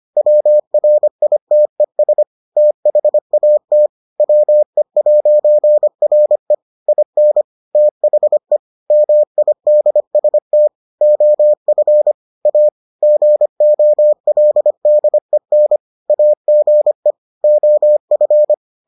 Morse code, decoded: WRITES THAT WE'RE IN THE MIDST OF A GOLDEN AGE OF

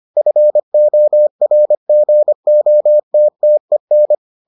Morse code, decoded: FORGOTTEN

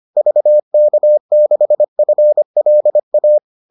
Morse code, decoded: VK6FLA

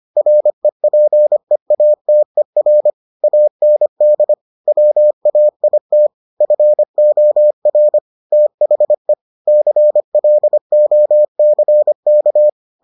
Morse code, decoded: REPEATER AND WAIT FOR THE CLOCK

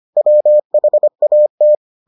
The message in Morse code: WHAT

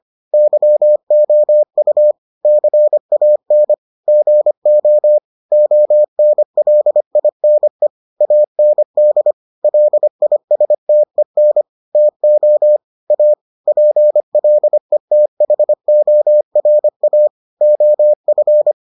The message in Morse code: YOU CAN GO ONLINE AND LISTEN TO A PLETHORA OF

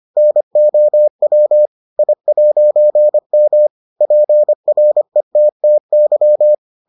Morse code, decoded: NOW I'M PRETTY

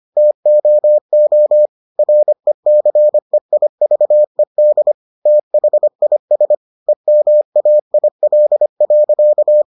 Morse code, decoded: TOO RECEIVED THIS EMAIL.

